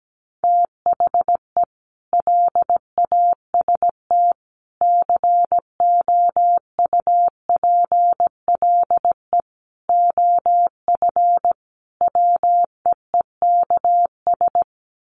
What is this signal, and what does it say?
Morse code, heard as a keyed tone, about 17 wpm, 700 Hz. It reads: THE LAST COUPLE OF WEEKS
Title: Sets of 5 Words Encore2 17wpm